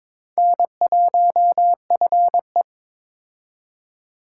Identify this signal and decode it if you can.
Morse code, heard as a keyed tone, about 21 wpm, 700 Hz. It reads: N1FE
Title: K1USN SST QSO Practice - 22wpm